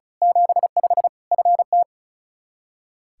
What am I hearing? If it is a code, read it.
Morse code, heard as a keyed tone, about 34 wpm, 700 Hz. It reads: 75 FT
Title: Practice QSO - Exchange 2 - Info - 35wpm